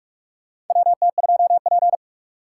Morse code, heard as a keyed tone, about 46 wpm, 700 Hz. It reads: WT2P
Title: USA Call Signs - 4 Characters - 2x Character Spacing - 45wpm